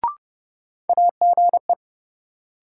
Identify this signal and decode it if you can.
Morse code, heard as a keyed tone, about 30 wpm, 700 Hz. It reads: AGE
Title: Sets of 3 Words - New Word Drill - 30wpm